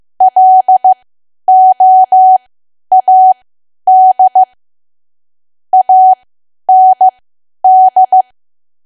Morse code, heard as a keyed tone, about 15 wpm, spaced as for 10 wpm, 750 Hz. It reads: LOAD AND
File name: Archive / 10 WPM / 10WPM